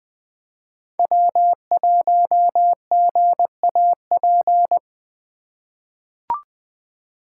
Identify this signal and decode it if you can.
Morse code, heard as a keyed tone, about 20 wpm, 700 Hz. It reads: W1GAP
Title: USA Call Signs - 5 Characters - 3x Character Spacing - 20wpm